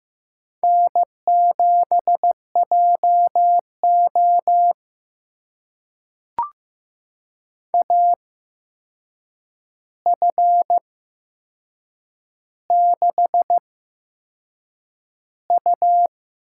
Morse code, heard as a keyed tone, about 15 wpm, spaced as for 6 wpm, 700 Hz. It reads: N7JO AF6U
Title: USA Call Signs - 4 Characters - 8x Character Spacing - 15wpm